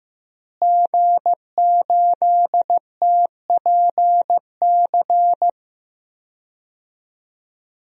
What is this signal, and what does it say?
Morse code, heard as a keyed tone, about 15 wpm, 700 Hz. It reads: G8TPC
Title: DXCC-100 Call Signs - 5 Characters - Repeated 3x - 15wpm